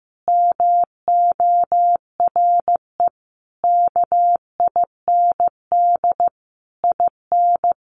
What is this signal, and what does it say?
Morse code, heard as a keyed tone, about 14 wpm, 700 Hz. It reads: MORE KIND IN
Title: Sentences from Top 1100 Words 15wpm